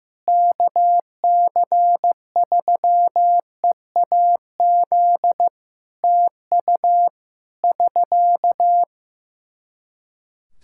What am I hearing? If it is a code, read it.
Morse code, heard as a keyed tone, about 15 wpm, 700 Hz. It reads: KC3EAZ TU <SK>
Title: Practice QSO - Exchange 4 - Ending - 15wpm